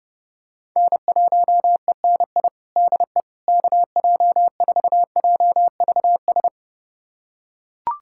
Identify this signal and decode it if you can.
Morse code, heard as a keyed tone, about 30 wpm, 700 Hz. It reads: N1ENI DE KJ4JVS
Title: USA Callsign DE Callsign 30wpm